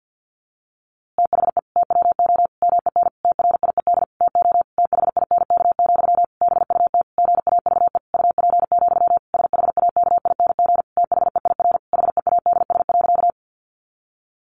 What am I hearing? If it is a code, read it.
Morse code, heard as a keyed tone, about 50 wpm, 700 Hz. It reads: THE TWO MEN TRIED TO THINK, BUT GAVE UP, SHAKING THEIR HEADS.
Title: A Sound Of Thunder 50wpm